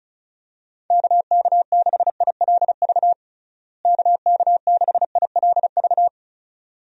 Morse code, heard as a keyed tone, about 36 wpm, 700 Hz. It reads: KK6ILV KK6ILV
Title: USA Call Signs - 6 Characters - Repeated 2x - 35wpm